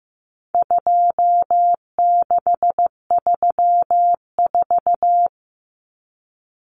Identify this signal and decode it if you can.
Morse code, heard as a keyed tone, about 15 wpm, 700 Hz. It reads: 2634
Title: Numbers - 4-Digit 15wpm